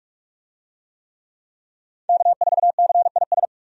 Morse code, heard as a keyed tone, about 43 wpm, 700 Hz. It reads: K4CIS
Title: USA Call Signs - 5 Characters - 45wpm